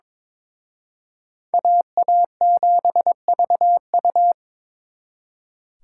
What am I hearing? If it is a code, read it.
Morse code, heard as a keyed tone, about 27 wpm, 700 Hz. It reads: AA7VU
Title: Speed Racing - USA 2x2 Call Signs 22wpm